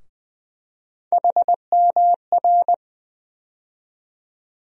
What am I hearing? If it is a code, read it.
Morse code, heard as a keyed tone, about 20 wpm, 700 Hz. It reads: HMR
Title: USA Call Sign Postfixes - 3 Letters 20wpm